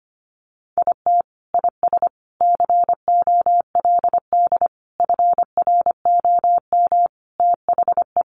Morse code, heard as a keyed tone, about 25 wpm, 700 Hz. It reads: IT IS COLD FROM THE
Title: Sentences from Top 1400 Words 25wpm